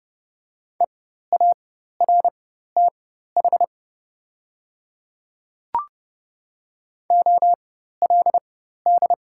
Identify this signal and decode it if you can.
Morse code, heard as a keyed tone, about 30 wpm, spaced as for 14 wpm, 700 Hz. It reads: EARTH OLD
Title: 4x Character Spacing - Top 200 Words - 30wpm